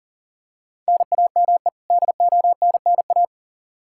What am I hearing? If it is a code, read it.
Morse code, heard as a keyed tone, about 40 wpm, 700 Hz. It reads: NAME DONNA
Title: Practice QSO - Exchange 1 - Beginning - 40wpm